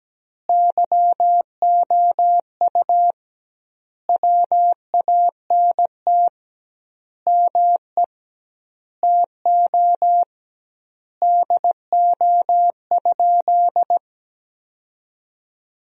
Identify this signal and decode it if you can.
Morse code, heard as a keyed tone, about 17 wpm, 700 Hz. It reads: YOU WANT ME TO DO?
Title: 2x Word-Spacing - Sentences from Top 200 Words - 17wpm